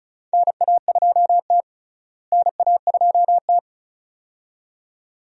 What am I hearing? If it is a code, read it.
Morse code, heard as a keyed tone, about 36 wpm, 700 Hz. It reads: NA2T NA2T
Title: USA Call Signs - 4 Characters - Repeated 2x - 35wpm